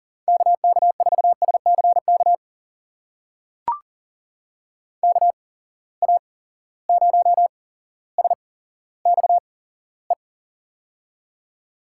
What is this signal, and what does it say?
Morse code, heard as a keyed tone, about 42 wpm, spaced as for 17 wpm, 700 Hz. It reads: KK4SCK KA0SXE
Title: USA Call Signs - 6 Characters - 8x Character Spacing - 40wpm